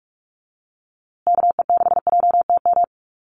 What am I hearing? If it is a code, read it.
Morse code, heard as a keyed tone, about 44 wpm, 700 Hz. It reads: KE6JTM
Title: USA Call Signs Repeated 3x 45wpm